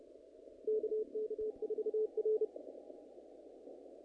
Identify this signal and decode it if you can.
Morse code, heard as a keyed tone, about 33 wpm, 430 Hz. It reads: KK4R
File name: contests / naqp-cw-k5tr-op-k5pi- -2000